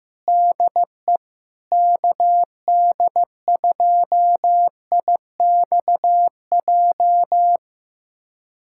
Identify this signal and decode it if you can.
Morse code, heard as a keyed tone, about 15 wpm, 700 Hz. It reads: DE KD2IXJ
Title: USA Callsign DE Callsign - 6 Characters - 15wpm